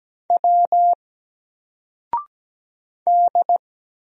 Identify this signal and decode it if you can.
Morse code, heard as a keyed tone, about 17 wpm, 700 Hz. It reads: W D
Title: Single Letter-Number Alt 17wpm